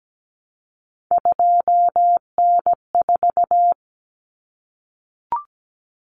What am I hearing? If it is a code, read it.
Morse code, heard as a keyed tone, about 17 wpm, 700 Hz. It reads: 2N4
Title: Common Cut Numbers - 3-Digit 17wpm